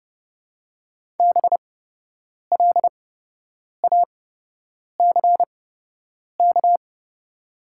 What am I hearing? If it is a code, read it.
Morse code, heard as a keyed tone, about 30 wpm, spaced as for 8 wpm, 700 Hz. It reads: BLACK
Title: 8x Character Spacing - Top 500 Words - 30wpm